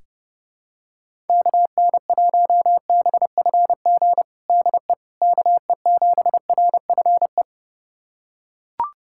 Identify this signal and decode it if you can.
Morse code, heard as a keyed tone, about 30 wpm, 700 Hz. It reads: KN1BFG DE KE7RFE
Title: USA Callsign DE Callsign - 6 Characters - 30wpm